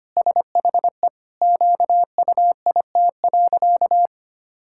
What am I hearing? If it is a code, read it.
Morse code, heard as a keyed tone, about 25 wpm, 700 Hz. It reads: SHE QUIT.
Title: Pangram Challenge v2 - 25wpm